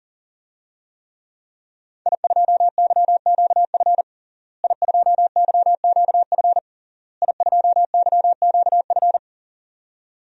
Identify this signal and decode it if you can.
Morse code, heard as a keyed tone, about 40 wpm, 700 Hz. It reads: I2YQF I2YQF I2YQF
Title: DXCC-100 Call Signs - 5 Characters - Repeated 3x - 40wpm